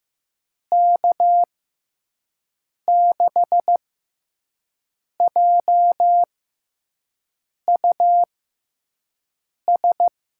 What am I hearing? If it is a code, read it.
Morse code, heard as a keyed tone, about 15 wpm, spaced as for 5 wpm, 700 Hz. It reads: K6JUS
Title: USA Call Signs - 6x Character Spacing - 15wpm